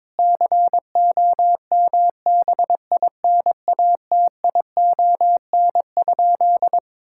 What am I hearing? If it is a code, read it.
Morse code, heard as a keyed tone, about 22 wpm, 700 Hz. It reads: COMBINATION?
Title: Sentences from Top 1000 Words - Review Part 6 - 22wpm